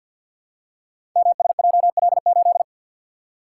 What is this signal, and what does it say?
Morse code, heard as a keyed tone, about 52 wpm, 700 Hz. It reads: MSJL8
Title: Code Groups - 5 Letter:Numbers - 50wpm